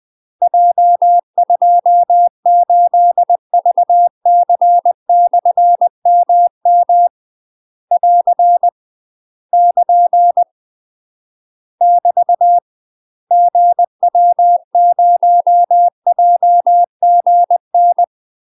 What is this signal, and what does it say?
Morse code, heard as a keyed tone, about 20 wpm, 700 Hz. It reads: J28VC/MM <AR> <KN> <BT> GW0JGN
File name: UK MASTER QSOs TRAINING/20WPM